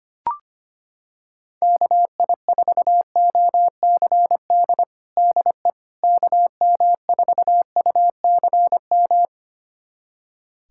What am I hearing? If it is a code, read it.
Morse code, heard as a keyed tone, about 25 wpm, 700 Hz. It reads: KI4OCD DE KM4UCM
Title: USA Callsign DE Callsign - 6 Characters - 25wpm